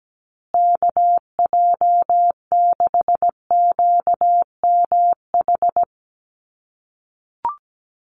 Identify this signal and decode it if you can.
Morse code, heard as a keyed tone, about 17 wpm, 700 Hz. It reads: KJ6QMH
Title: USA Call Signs Repeated 2x 17wpm